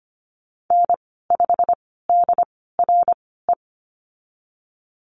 Morse code, heard as a keyed tone, about 25 wpm, spaced as for 16 wpm, 700 Hz. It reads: N5DRE
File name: Lesson 15g - Number 5 - Call Signs 25wpm